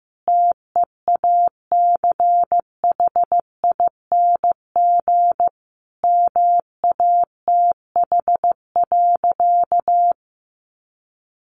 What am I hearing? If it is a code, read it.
Morse code, heard as a keyed tone, about 14 wpm, 700 Hz. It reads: TEACHING MATH.
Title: Easy Phrases 15wpm